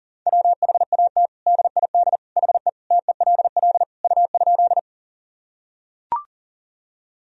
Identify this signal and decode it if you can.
Morse code, heard as a keyed tone, about 38 wpm, 700 Hz. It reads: WHAT DID HE TELL U?
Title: Sentences from Top 200 Words and Abbreviations 40wpm